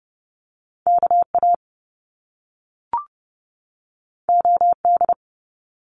Morse code, heard as a keyed tone, about 30 wpm, 700 Hz. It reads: KA OD
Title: Top Two Letter Combinations in Words 30wpm